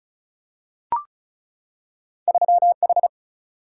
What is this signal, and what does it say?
Morse code, heard as a keyed tone, about 36 wpm, 700 Hz. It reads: 3H
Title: Single-Letter Single-Number 35wpm